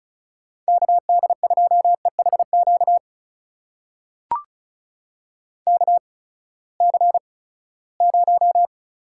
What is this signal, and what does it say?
Morse code, heard as a keyed tone, about 36 wpm, spaced as for 20 wpm, 700 Hz. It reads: KD2EHQ KC0
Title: USA Call Signs - 6 Characters - 8x Character Spacing - 35wpm